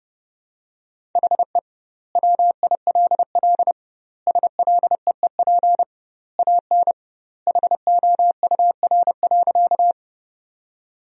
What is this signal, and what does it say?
Morse code, heard as a keyed tone, about 32 wpm, 700 Hz. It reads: HE WILL SLEEP AN HOUR.
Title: 2x Word-Spacing - Sentences from Top 500 Words - 30wpm